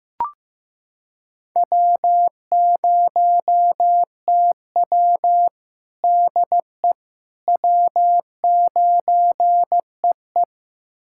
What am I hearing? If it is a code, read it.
Morse code, heard as a keyed tone, about 15 wpm, 700 Hz. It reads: W0TW DE W9EE
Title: USA Callsign DE Callsign - 4 Characters - 15wpm